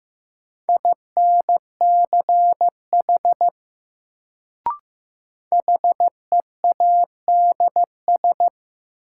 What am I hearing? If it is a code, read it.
Morse code, heard as a keyed tone, about 15 wpm, 700 Hz. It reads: INCH HEADS
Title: Sentences from Top 1000 Words - Review Part 3 - New Word Drill - 15wpm